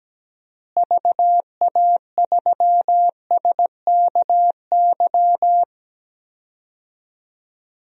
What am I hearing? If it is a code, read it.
Morse code, heard as a keyed tone, about 17 wpm, 700 Hz. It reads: VA3SKY
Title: DXCC-100 Call Signs - 6 Characters - Repeated 3x - 17wpm